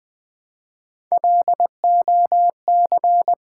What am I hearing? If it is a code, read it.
Morse code, heard as a keyed tone, about 20 wpm, 700 Hz. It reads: LOC
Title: Build A Word - 1 Letter At A Time - 5-Letter Words - 20wpm